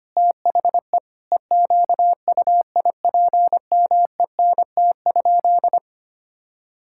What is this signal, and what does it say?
Morse code, heard as a keyed tone, about 25 wpm, 700 Hz. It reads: THE EQUIPMENT?
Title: Sentences from Top 1000 Words - Review Part 5 - 25wpm